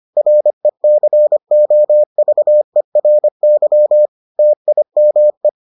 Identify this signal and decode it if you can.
Morse code, heard as a keyed tone, about 25 wpm, 600 Hz. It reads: RECOVERY TIME